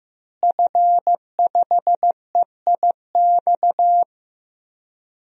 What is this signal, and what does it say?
Morse code, heard as a keyed tone, about 15 wpm, 700 Hz. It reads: F5EIX
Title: DXCC-100 Call Signs - 5 Characters - Repeated 2x - 15wpm